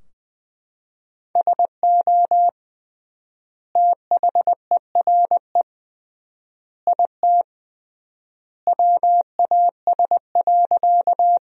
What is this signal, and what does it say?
Morse code, heard as a keyed tone, about 21 wpm, 700 Hz. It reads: SO THERE IT WAS.
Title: 3x Word-Spacing - Sentences from Top 100 Words - 20wpm